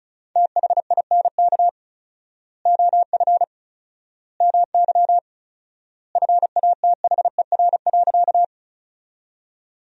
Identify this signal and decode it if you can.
Morse code, heard as a keyed tone, about 34 wpm, 700 Hz. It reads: THINK OF MY FATHER.
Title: 4x Word-Spacing - Sentences from Top 300 Words - 35wpm